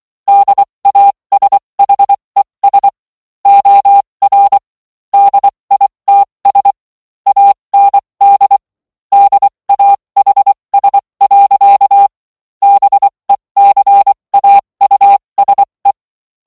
Morse code, heard as a keyed tone, about 24 wpm, spaced as for 20 wpm, 800 Hz. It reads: DASHES OR DITS AND DAHS. BECAUSE